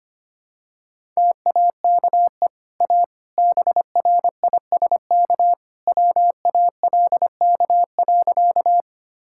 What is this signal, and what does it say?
Morse code, heard as a keyed tone, about 25 wpm, 700 Hz. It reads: TAKE A BRISK WALK.
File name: Easy Sentences with Hints - 25wpm